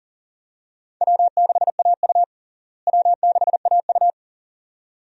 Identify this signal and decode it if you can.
Morse code, heard as a keyed tone, about 37 wpm, 700 Hz. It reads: W6AU W6AU
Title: USA Call Signs - 4 Characters - Repeated 2x - 40wpm